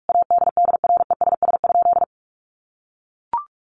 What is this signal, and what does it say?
Morse code, heard as a keyed tone, about 44 wpm, 700 Hz. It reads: ADDRESS?
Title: Sentences from Top 1000 Words 45wpm